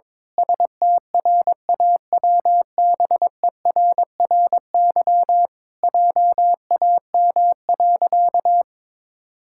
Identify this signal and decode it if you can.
Morse code, heard as a keyed tone, about 21 wpm, 700 Hz. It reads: STRAWBERRY JAM.
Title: Pangram Challenge v2 - 22wpm